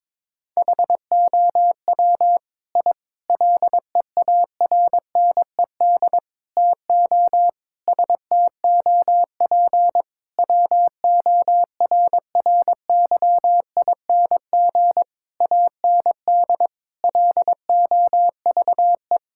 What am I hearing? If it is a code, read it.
Morse code, heard as a keyed tone, about 21 wpm, 700 Hz. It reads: HOW I LEARNED TO STOP WORRYING AND LOVE
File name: practice / 22wpm / Sentences with 8-Word Phrases - 22wpm